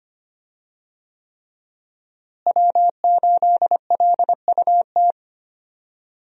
Morse code, heard as a keyed tone, about 25 wpm, 700 Hz. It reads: W8LUT
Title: USA Call Signs - 5 Characters - Repeated 3x - 25wpm